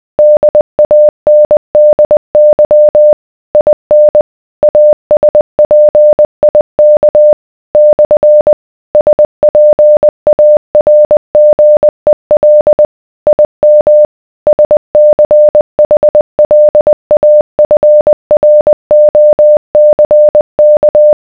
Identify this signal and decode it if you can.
Morse code, heard as a keyed tone, about 20 wpm, 600 Hz. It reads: DANDY IN ASPIK / SPARGEL IM SCHLAFROCK